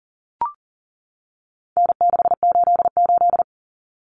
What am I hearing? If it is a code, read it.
Morse code, heard as a keyed tone, about 40 wpm, 700 Hz. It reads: N688
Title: Common Cut Numbers - 4-Digit 40wpm